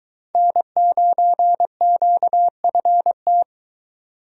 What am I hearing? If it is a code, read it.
Morse code, heard as a keyed tone, about 21 wpm, 700 Hz. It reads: N9QFT